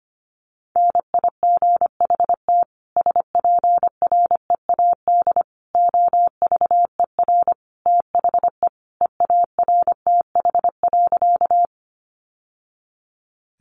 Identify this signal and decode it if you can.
Morse code, heard as a keyed tone, about 25 wpm, 700 Hz. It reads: NIGHT SPREAD OVER THE EARTH.
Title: Sentences from Top 1200 Words 25wpm